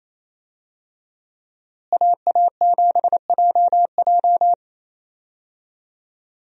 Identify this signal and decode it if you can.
Morse code, heard as a keyed tone, about 28 wpm, 700 Hz. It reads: AA7JJ
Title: USA Call Signs - 5 Characters - Repeated 3x - 28wpm